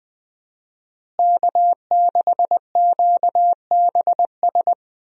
Morse code, heard as a keyed tone, about 21 wpm, 700 Hz. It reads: K6QBS
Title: USA Call Signs - 5 Characters - Repeated 2x - 20wpm